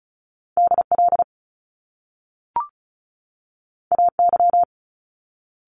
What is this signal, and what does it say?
Morse code, heard as a keyed tone, about 36 wpm, 700 Hz. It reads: D L A Y
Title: Top Two Letter Combinations in Words 35wpm